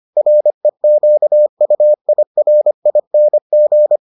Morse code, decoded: REQUIRING

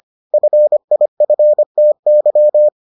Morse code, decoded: FIFTY

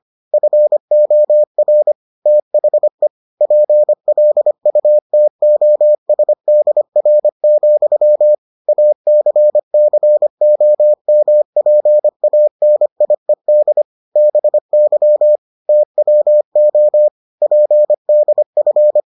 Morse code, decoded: FOR THE PLUTOSDR, ACCOMPANIED BY TWO PDF

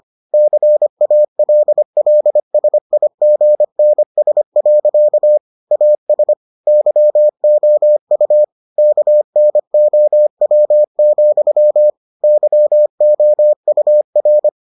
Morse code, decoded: CALLSIGNS. AS YOU KNOW, YOUR

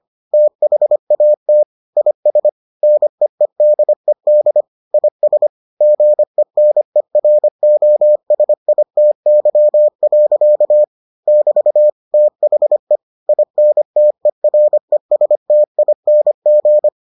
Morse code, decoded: THAT IS NEEDED IS GENEROSITY. <BT> THE INTERESTING

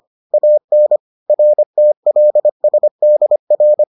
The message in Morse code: AN RTLSDR

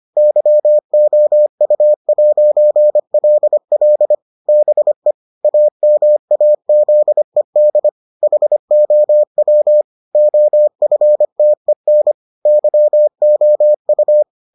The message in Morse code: YOU'LL BE AMAZED HOW OFTEN YOU

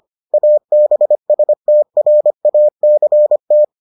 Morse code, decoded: ABSTRACT

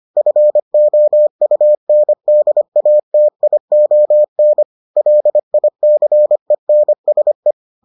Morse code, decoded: FOUNDATION LICENSE

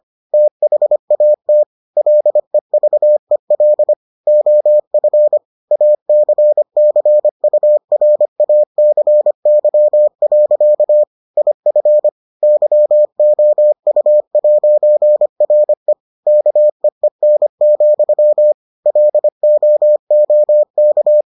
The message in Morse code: THAT LEVEL OF ACCURACY. IF YOU'RE KEEN, LOOK

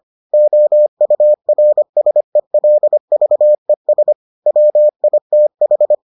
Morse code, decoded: OURSELVES WITH

